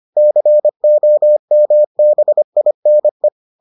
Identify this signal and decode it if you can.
Morse code, heard as a keyed tone, about 25 wpm, 600 Hz. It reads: COMBINE